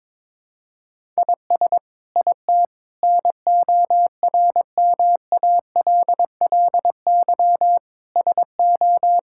Morse code, heard as a keyed tone, about 22 wpm, 700 Hz. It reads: IS IT NORMALLY SO
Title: Sentences from Top 1000 Words - Review Part 6 - 22wpm